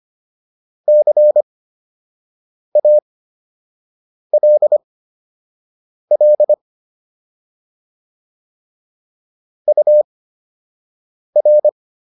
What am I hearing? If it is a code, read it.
Morse code, heard as a keyed tone, about 25 wpm, spaced as for 6 wpm, 600 Hz. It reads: CALL UR
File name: QSO 105 06